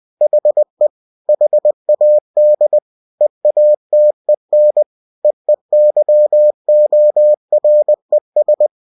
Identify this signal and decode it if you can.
Morse code, heard as a keyed tone, about 20 wpm, 600 Hz. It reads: HE HAD EATEN EEYORES
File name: cw-books / winnie-the-pooh / winnie-20wpm-page-0050